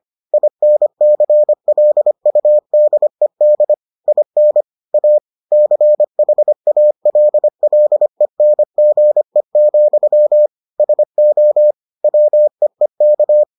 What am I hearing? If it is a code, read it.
Morse code, heard as a keyed tone, about 25 wpm, 600 Hz. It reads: INCLUDED IN A CHALLENGE, SO WEEK